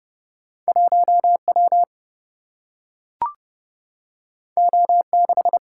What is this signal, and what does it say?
Morse code, heard as a keyed tone, about 30 wpm, 700 Hz. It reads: 1W O6
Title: Single-Letter Single-Number 30wpm